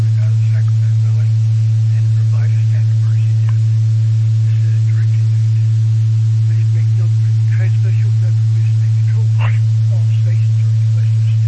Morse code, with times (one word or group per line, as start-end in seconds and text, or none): none